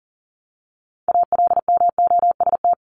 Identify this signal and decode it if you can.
Morse code, heard as a keyed tone, about 40 wpm, 700 Hz. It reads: ALMOST
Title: Top 501-750 Words 40wpm